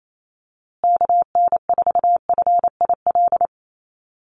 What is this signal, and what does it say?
Morse code, heard as a keyed tone, about 28 wpm, 700 Hz. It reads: KN4FIL
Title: USA Call Signs Repeated 3x 28wpm